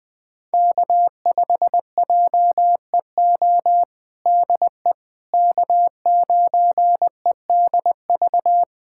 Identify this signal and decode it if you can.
Morse code, heard as a keyed tone, about 20 wpm, 700 Hz. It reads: K5JEO DE K9EDV
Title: USA Callsign DE Callsign - 5 Characters - 20wpm